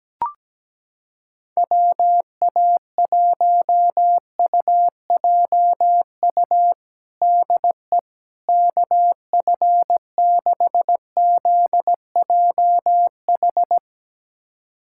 Morse code, heard as a keyed tone, about 17 wpm, 700 Hz. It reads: WA1UJU DE KF6ZJH
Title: USA Callsign DE Callsign 17wpm